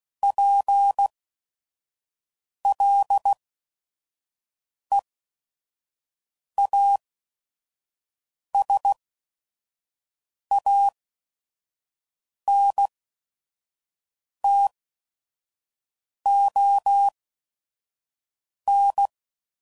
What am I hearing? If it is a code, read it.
Morse code, heard as a keyed tone, about 16 wpm, spaced as for 5 wpm, 800 Hz. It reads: PLEASANTON
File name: CW Code Groups @5WPM / fromSM's / Cities